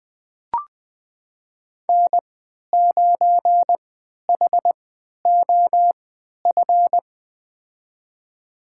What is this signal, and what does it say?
Morse code, heard as a keyed tone, about 21 wpm, spaced as for 12 wpm, 700 Hz. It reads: N9HOF
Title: USA Call Signs - 5 Characters - 3x Character Spacing - 20wpm